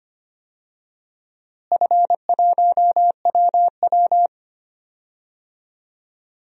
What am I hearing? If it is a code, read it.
Morse code, heard as a keyed tone, about 25 wpm, 700 Hz. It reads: F1WW
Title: DXCC-100 Call Signs - 4 Characters - Repeated 3x - 25wpm